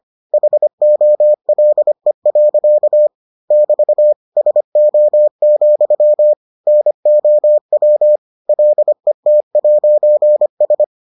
Morse code, heard as a keyed tone, about 25 wpm, 600 Hz. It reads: HOLE. <BT> SO, NOW LET'S